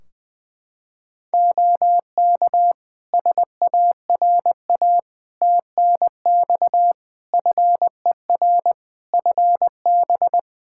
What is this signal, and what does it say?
Morse code, heard as a keyed tone, about 20 wpm, 700 Hz. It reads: OK SARA TNX FER FB
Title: Practice QSO - Exchange 4 - Ending - 20wpm